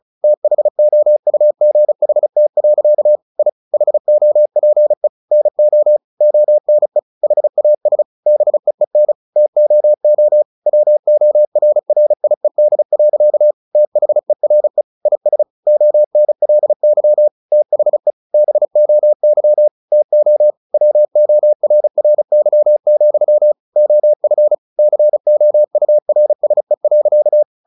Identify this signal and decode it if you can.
Morse code, heard as a keyed tone, about 36 wpm, 600 Hz. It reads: THOUGHT. I HOPE NO ONE HAS BEEN TOO WORRIED. THERE IS ONLY THE BOY TO WORRY, OF COURSE.
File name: the old man and the sea 2 35wpm